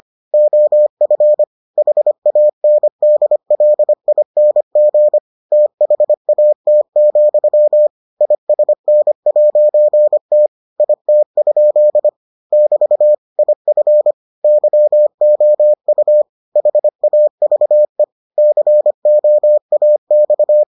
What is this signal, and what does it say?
Morse code, heard as a keyed tone, about 25 wpm, 600 Hz. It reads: OF HANDLING THAT, ISN'T IT? <BT> IF YOU HAVE COAX